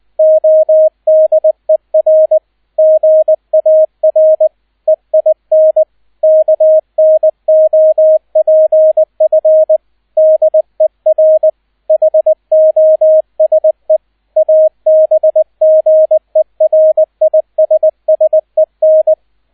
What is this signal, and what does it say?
Morse code, heard as a keyed tone, about 20 wpm, 630 Hz, with radio noise. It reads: ODER GAR EIN KNOPF DER HOSE ABGERISSEN